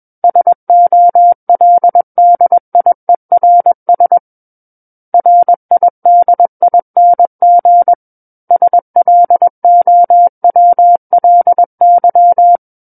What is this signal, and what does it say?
Morse code, heard as a keyed tone, about 21 wpm, 700 Hz. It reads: SOLDIERS RIDING SLOWLY
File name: Book One / 12-Chapter 12 - 21 WPM